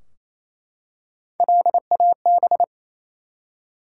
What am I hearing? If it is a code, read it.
Morse code, heard as a keyed tone, about 29 wpm, 700 Hz. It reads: LAB
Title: Three Letter Words - 28wpm